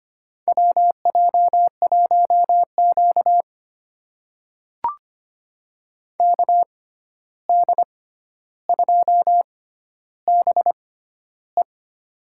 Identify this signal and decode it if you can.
Morse code, heard as a keyed tone, about 25 wpm, spaced as for 12 wpm, 700 Hz. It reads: WJ1Q KD2BE
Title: USA Call Signs - 6x Character Spacing - 25wpm